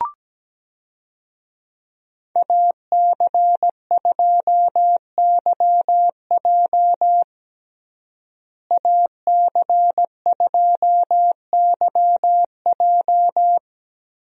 Morse code, heard as a keyed tone, about 17 wpm, 700 Hz. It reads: AC2YJ AC2YJ
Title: USA Call Signs - 5 Characters - Repeated 2x - 17wpm